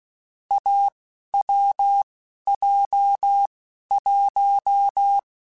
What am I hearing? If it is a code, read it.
Morse code, heard as a keyed tone, about 16 wpm, spaced as for 11 wpm, 800 Hz. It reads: AWJ1
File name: Short Runs / Short Runs-mp3s / awj1ndb6gz7 @10